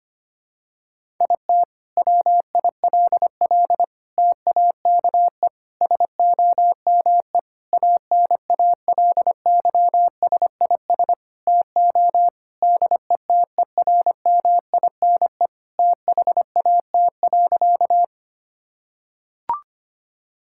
Morse code, heard as a keyed tone, about 25 wpm, 700 Hz. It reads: IT WILL TAKE SOME ANALYSIS TO DETERMINE THAT.
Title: Sentences from Top 1000 Words - Review Part 5 - 25wpm